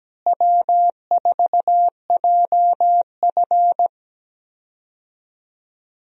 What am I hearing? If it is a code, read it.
Morse code, heard as a keyed tone, about 17 wpm, 700 Hz. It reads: W4JF
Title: USA Call Signs - 4 Characters - Repeated 3x - 17wpm